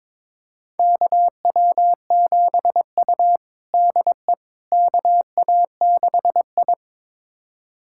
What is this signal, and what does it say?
Morse code, heard as a keyed tone, about 22 wpm, 700 Hz. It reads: KW7U DE KA6I
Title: USA Callsign DE Callsign - 4 Characters - 22wpm